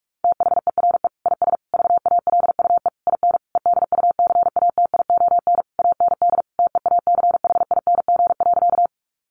Morse code, heard as a keyed tone, about 44 wpm, 700 Hz. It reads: THERE IS VALUE IN EDUCATION AND TEACHING.
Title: Sentences from Top 1100 Words 45wpm